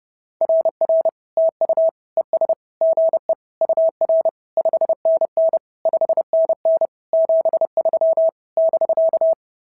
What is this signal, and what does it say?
Morse code, heard as a keyed tone, about 30 wpm, 650 Hz. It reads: RR TU ES GE UR 5NN 5NN 73 <BK>
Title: SOTA QSO Practice 30wpm